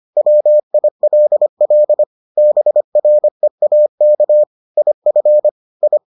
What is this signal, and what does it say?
Morse code, heard as a keyed tone, about 25 wpm, 600 Hz. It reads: WILL BREAK IF I